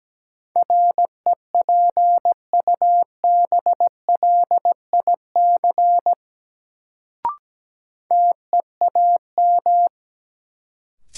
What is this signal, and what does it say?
Morse code, heard as a keyed tone, about 17 wpm, 700 Hz. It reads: REPUBLIC TEAM
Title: Sentences from Top 1500 Words - New Word Drill - 17wpm